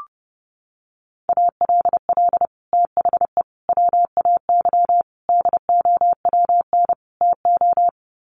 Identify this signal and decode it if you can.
Morse code, heard as a keyed tone, about 28 wpm, 700 Hz. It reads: ALL THE WAY DOWN TO
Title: Sets of 5 Words Encore2 30wpm